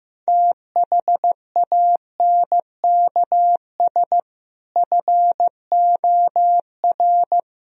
Morse code, heard as a keyed tone, about 15 wpm, 700 Hz. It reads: THANKS FOR
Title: Speed-Racing - Sets of 5 Words 15wpm